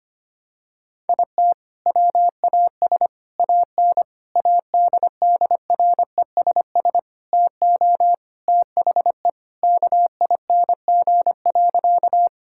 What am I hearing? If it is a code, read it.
Morse code, heard as a keyed tone, about 25 wpm, 700 Hz. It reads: IT WAS AN ADDRESS TO THE KING.
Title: Sentences from Top 1000 Words - Review Part 6 - 25wpm